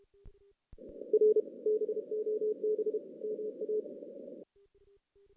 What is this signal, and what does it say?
Morse code, heard as a keyed tone, about 30 wpm, 440 Hz. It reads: R BOB MA
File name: 2016 JAN CW NAQP / 2016 Jan naqp k5ot - 0-4